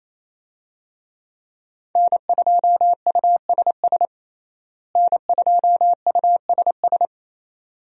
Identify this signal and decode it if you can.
Morse code, heard as a keyed tone, about 28 wpm, 700 Hz. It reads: N2USS N2USS
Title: USA Call Signs - 5 Characters - Repeated 2x - 28wpm